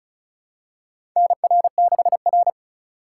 Morse code, heard as a keyed tone, about 34 wpm, 700 Hz. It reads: NR6R
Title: USA Call Signs - 4 Characters - 4x Character Spacing - 35wpm